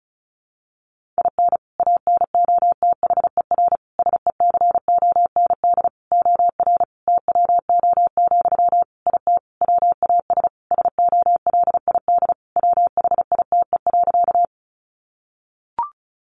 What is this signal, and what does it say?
Morse code, heard as a keyed tone, about 34 wpm, 700 Hz. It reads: IN ANOTHER SECOND OR TWO, IT WAS SOLID WHITE.
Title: Sentences from Top 1500 Words 35wpm